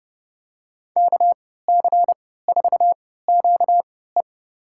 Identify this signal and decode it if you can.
Morse code, heard as a keyed tone, about 30 wpm, spaced as for 17 wpm, 700 Hz. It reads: KC4QE
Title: USA Call Signs - 5 Characters - 3x Character Spacing - 30wpm